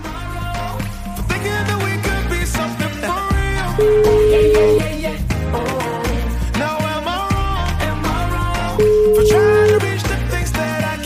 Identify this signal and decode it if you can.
Ringback tone (Europe, 425 Hz), with noise.